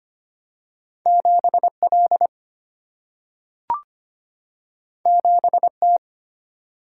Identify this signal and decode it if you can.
Morse code, heard as a keyed tone, about 25 wpm, 700 Hz. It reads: 7L 7T
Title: Single-Letter Single-Number 25wpm